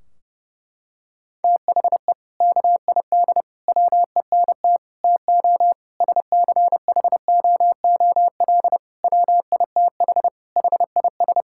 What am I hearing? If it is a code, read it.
Morse code, heard as a keyed tone, about 32 wpm, 700 Hz. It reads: THE KID WENT TO SCHOOL WITH HIS
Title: Sentences from Top 1600 Words 30wpm